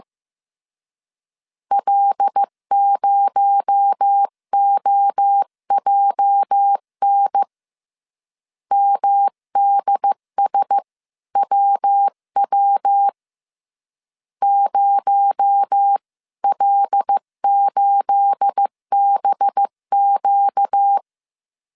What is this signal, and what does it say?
Morse code, heard as a keyed tone, about 15 wpm, 800 Hz. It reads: L0OJN MDSWW 0L8BQ
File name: CW / DET / 1 9 1500C S B 14wpm 3302